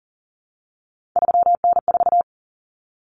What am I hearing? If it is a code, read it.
Morse code, heard as a keyed tone, about 40 wpm, 700 Hz. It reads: 3N4
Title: Common Cut Numbers - 3-Digit 40wpm